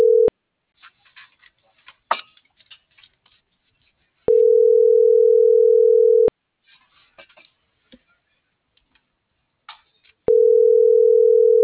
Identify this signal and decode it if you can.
Ringback tone (North America), 440 + 480 Hz.